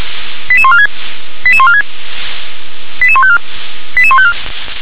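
Five-tone selective call (ZVEI), heard as quick runs of five tones, tones of about 70 ms. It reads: 88146 88146 88154 88154
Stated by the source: ZVEI five-tone selective call received by SDR.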